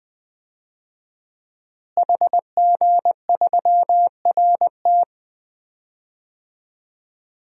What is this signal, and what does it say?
Morse code, heard as a keyed tone, about 20 wpm, 700 Hz. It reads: HG3RT
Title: DXCC-100 Call Signs - 5 Characters - Repeated 3x - 20wpm